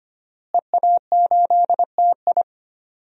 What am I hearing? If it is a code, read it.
Morse code, heard as a keyed tone, about 25 wpm, 700 Hz. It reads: EA8TI
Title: DXCC-100 Call Signs - 5 Characters - Repeated 2x - 25wpm